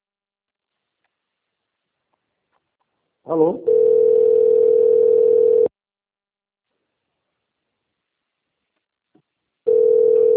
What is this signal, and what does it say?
Ringback tone (North America), 440 + 480 Hz, with speech.
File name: out- - - - - .799